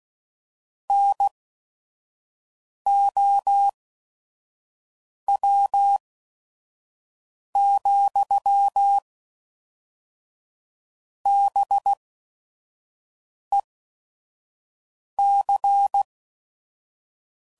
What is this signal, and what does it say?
Morse code, heard as a keyed tone, about 16 wpm, spaced as for 5 wpm, 800 Hz. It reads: NOW, BEC
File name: CW Code Groups @5WPM / Technical Subjects / Gates Intro